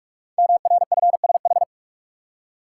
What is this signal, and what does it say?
Morse code, heard as a keyed tone, about 44 wpm, 700 Hz. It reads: MRFSH
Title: Code Groups - 5 Letters - 45wpm